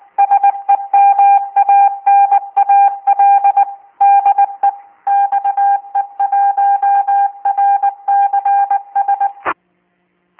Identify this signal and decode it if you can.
Morse code, heard as a keyed tone, about 20 wpm, 800 Hz. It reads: SEMANAL DE XE1RCS